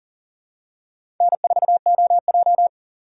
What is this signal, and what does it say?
Morse code, heard as a keyed tone, about 40 wpm, 700 Hz. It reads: N4OJ
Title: USA Call Signs - 4 Characters - 3x Character Spacing - 40wpm